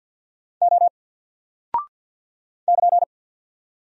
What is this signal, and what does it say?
Morse code, heard as a keyed tone, about 48 wpm, 700 Hz. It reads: O <KN>
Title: Single Character (Letter:Number:Punctuation:Prosign) Alt 50wpm